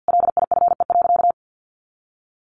Morse code, heard as a keyed tone, about 50 wpm, 700 Hz. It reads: LIFE.
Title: Sentences from Top 600 Words 50wpm